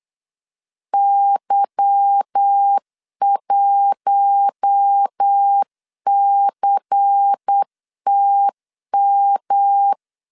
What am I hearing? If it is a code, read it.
Morse code, heard as a keyed tone, about 8 wpm, 800 Hz. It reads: Y1CTM
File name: LEZIONI / 8 YJ45 / 8 Y 8wpm 3245